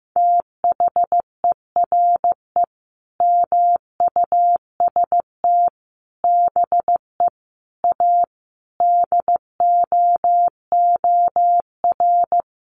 Morse code, heard as a keyed tone, about 15 wpm, 700 Hz. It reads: THERE MUST BE A DOOR
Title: Sentences from Top 1400 Words 15wpm